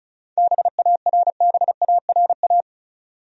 Morse code, heard as a keyed tone, about 34 wpm, 700 Hz. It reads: BARBARA
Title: USA Common Names - 35wpm